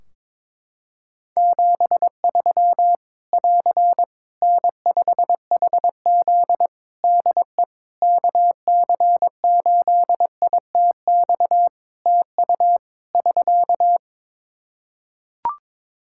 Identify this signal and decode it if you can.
Morse code, heard as a keyed tone, about 22 wpm, 700 Hz. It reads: 73 <AR> N5HZ DE KC8ITX TU <SK>
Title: Practice QSO - Exchange 4 - Ending - 22wpm